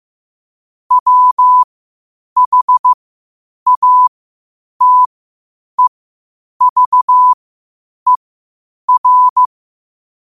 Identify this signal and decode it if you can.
Morse code, heard as a keyed tone, about 15 wpm, spaced as for 8 wpm, 1000 Hz. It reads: WHATEVER
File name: qotdfiles / 8wpm